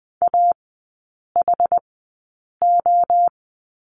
Morse code, heard as a keyed tone, about 20 wpm, spaced as for 8 wpm, 700 Hz. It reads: AHO
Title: 060 Lesson 11e - Letter h - Three Characters 20wpm